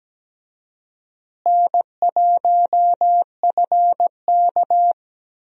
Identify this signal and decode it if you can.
Morse code, heard as a keyed tone, about 17 wpm, 700 Hz. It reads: N1FK
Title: USA Call Signs - 4 Characters - Repeated 3x - 17wpm